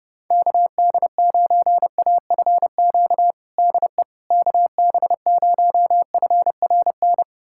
Morse code, heard as a keyed tone, about 30 wpm, 700 Hz. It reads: KD9AFQ DE KB0FRN
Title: USA Callsign DE Callsign - 6 Characters - 30wpm